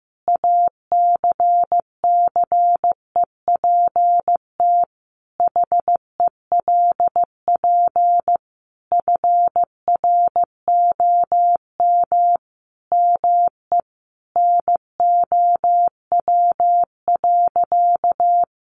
Morse code, heard as a keyed tone, about 15 wpm, 700 Hz. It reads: ACCEPT HELP FROM ME NOW.
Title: Sentences from Top 1300 Words 15wpm